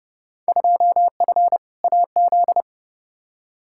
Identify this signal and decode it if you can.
Morse code, heard as a keyed tone, about 32 wpm, 700 Hz. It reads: 2F AZ
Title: Field Day Exchanges - 30wpm